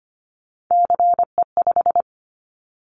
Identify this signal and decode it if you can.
Morse code, heard as a keyed tone, about 25 wpm, 700 Hz. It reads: CE5
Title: Callsign Prefixes for Top 100 DXCC - 25wpm